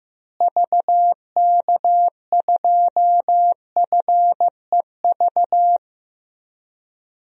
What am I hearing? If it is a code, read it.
Morse code, heard as a keyed tone, about 15 wpm, 700 Hz. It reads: VK2FEV
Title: DXCC-100 Call Signs - 6 Characters - Repeated 2x - 15wpm